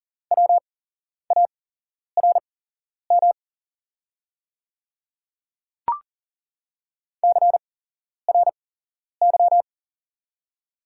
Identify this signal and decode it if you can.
Morse code, heard as a keyed tone, about 39 wpm, spaced as for 11 wpm, 690 Hz. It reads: WARM CRY
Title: 8x Character Spacing - Top 500 Words - 40wpm